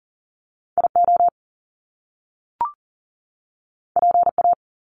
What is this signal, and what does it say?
Morse code, heard as a keyed tone, about 40 wpm, 700 Hz. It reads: IO PA
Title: Top Two Letter Combinations in Words 40wpm